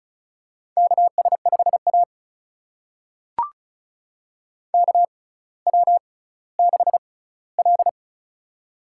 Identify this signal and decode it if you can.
Morse code, heard as a keyed tone, about 36 wpm, spaced as for 18 wpm, 700 Hz. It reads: KS5A KW6L
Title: USA Call Signs - 4 Characters - 6x Character Spacing - 35wpm